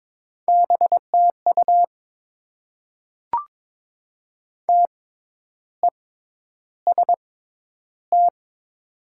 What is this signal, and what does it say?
Morse code, heard as a keyed tone, about 22 wpm, spaced as for 10 wpm, 700 Hz. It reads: BTU TEST
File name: Top 90 QSO Elements - 6x Character Spacing - 22wpm